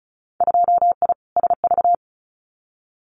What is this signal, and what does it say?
Morse code, heard as a keyed tone, about 36 wpm, 700 Hz. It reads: 2I SV
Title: Winter Field Day Exchanges Encore 35wpm